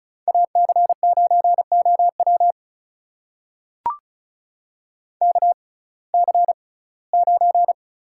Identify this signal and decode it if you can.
Morse code, heard as a keyed tone, about 35 wpm, spaced as for 21 wpm, 700 Hz. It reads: AC9OW KC9
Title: USA Call Signs - 5 Characters - 6x Character Spacing - 35wpm